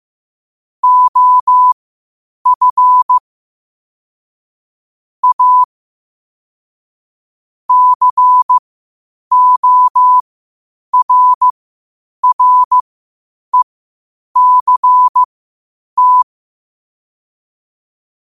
Morse code, heard as a keyed tone, about 15 wpm, spaced as for 8 wpm, 1000 Hz. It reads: OF A CORRECT
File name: qotdfiles / 8wpm